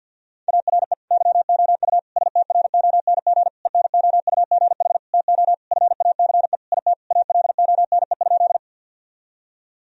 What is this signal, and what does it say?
Morse code, heard as a keyed tone, about 52 wpm, 700 Hz. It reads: ARE YOU STRONG ENOUGH TO FACE IT ALONE?
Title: Sentences from Top 1000 Words - Review Part 3 - 50wpm